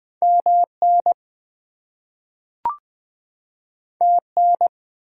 Morse code, heard as a keyed tone, about 20 wpm, 700 Hz. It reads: MN TN
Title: US States and Canadian Provinces 20wpm